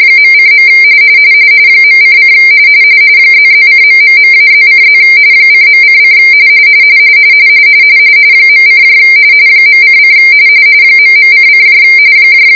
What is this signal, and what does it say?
RTTY, about 45.45 baud. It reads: .:. .:: ++:...-+:+ :.:: .: .:: ..::